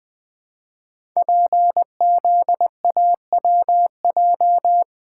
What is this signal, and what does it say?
Morse code, heard as a keyed tone, about 20 wpm, 700 Hz. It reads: PZAWJ
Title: Code Groups - 5 Letter:Numbers - 20wpm